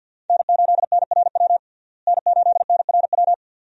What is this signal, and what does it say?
Morse code, heard as a keyed tone, about 46 wpm, 700 Hz. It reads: N8NRW N8NRW
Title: USA Call Signs - 5 Characters - Repeated 2x - 50wpm